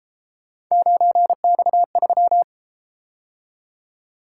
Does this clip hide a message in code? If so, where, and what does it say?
Yes: Morse code from 0.7 to 2.4 s, heard as a keyed tone, about 32 wpm, 700 Hz. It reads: 9X3